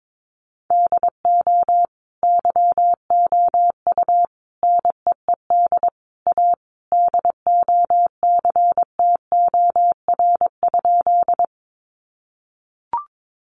Morse code, heard as a keyed tone, about 22 wpm, 700 Hz. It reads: DO YOU NEED A DOCTOR?
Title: Sentences from Top 1500 Words 22wpm